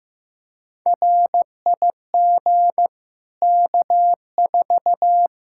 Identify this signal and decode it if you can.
Morse code, heard as a keyed tone, about 15 wpm, 700 Hz. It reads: RIG K4
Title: Practice QSO - Exchange 2 - Info - 15wpm